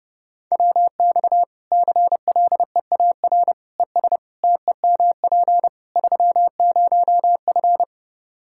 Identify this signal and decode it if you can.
Morse code, heard as a keyed tone, about 30 wpm, 700 Hz. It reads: WX CLEAR ES TEMP 30F
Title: Practice QSO - Exchange 2 - Info - 30wpm